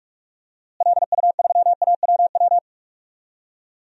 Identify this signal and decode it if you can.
Morse code, heard as a keyed tone, about 46 wpm, 700 Hz. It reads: LU3AWW
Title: DXCC-100 Call Signs - 6 Characters - Repeated 2x - 45wpm